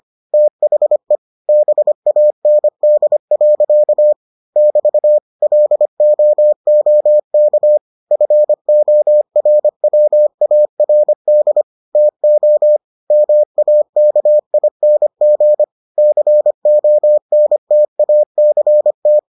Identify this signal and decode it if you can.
Morse code, heard as a keyed tone, about 25 wpm, 600 Hz. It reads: THE BAND. <BT> LOOK FORWARD TO MAKING CONTACT